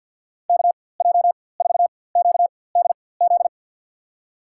Morse code, heard as a keyed tone, about 51 wpm, spaced as for 23 wpm, 700 Hz. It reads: KJ4QDZ
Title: USA Call Signs - 6 Characters - 4x Character Spacing - 50wpm